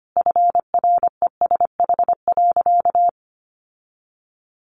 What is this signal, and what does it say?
Morse code, heard as a keyed tone, about 25 wpm, 700 Hz. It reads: FRESH.
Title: Sentences from Top 1500 Words 25wpm